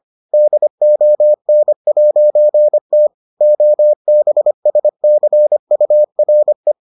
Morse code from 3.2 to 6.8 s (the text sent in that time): OBSCURE